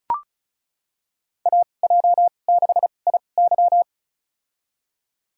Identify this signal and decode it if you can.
Morse code, heard as a keyed tone, about 34 wpm, spaced as for 25 wpm, 700 Hz. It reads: AJ6IY
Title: USA Call Signs - 5 Characters - 2x Character Spacing - 35wpm